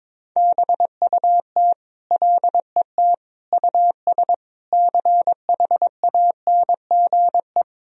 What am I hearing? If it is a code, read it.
Morse code, heard as a keyed tone, about 22 wpm, 700 Hz. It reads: BUT LET US CHANGE
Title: Sentences from Top 1000 Words - Review Part 1 - 22wpm